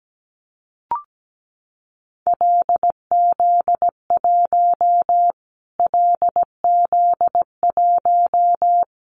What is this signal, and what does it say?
Morse code, heard as a keyed tone, about 17 wpm, 700 Hz. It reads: LZ1 LZ1
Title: Callsign Prefixes for Top 25 DXCC - Repeated 3x - 17wpm